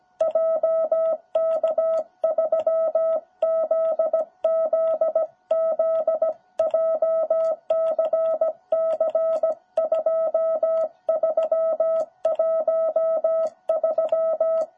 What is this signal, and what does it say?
Morse code, heard as a keyed tone, about 17 wpm, 660 Hz. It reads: JK3ZZZ JCC2313
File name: jh8cbh / 2021cw